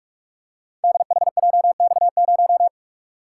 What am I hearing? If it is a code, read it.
Morse code, heard as a keyed tone, about 47 wpm, 700 Hz. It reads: DHJX0
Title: Code Groups - 5 Letter:Numbers - 45wpm